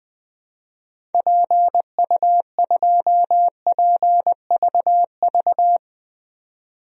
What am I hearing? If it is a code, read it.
Morse code, heard as a keyed tone, about 20 wpm, 700 Hz. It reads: PU2PVV
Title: DXCC-100 Call Signs - 6 Characters - 20wpm